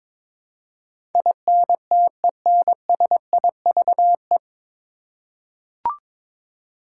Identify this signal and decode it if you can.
Morse code, heard as a keyed tone, about 22 wpm, 700 Hz. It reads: INTENSIVE
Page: Morse Code Ninja practice 22wpm